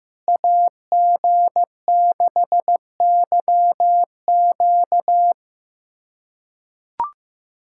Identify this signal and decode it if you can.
Morse code, heard as a keyed tone, about 15 wpm, 700 Hz. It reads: AG6YQ
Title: USA Callsign DE Callsign 15wpm